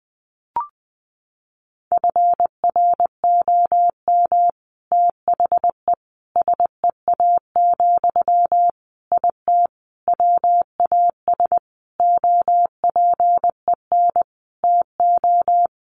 Morse code, heard as a keyed tone, about 20 wpm, 700 Hz. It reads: FROM THE SEA, IT WAS OPEN TO
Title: Sentences from Top 1500 Words 20wpm